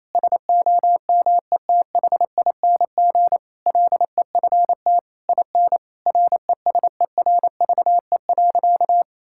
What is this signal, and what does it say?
Morse code, heard as a keyed tone, about 27 wpm, 700 Hz. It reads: SOMETHING LEFT IN RESERVE.
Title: Sentences from Top 1000 Words - Review Part 6 - 28wpm